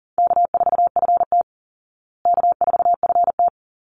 Morse code, heard as a keyed tone, about 40 wpm, 700 Hz. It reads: K4FT K4FT
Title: Call Signs Repeated 3x - 40wpm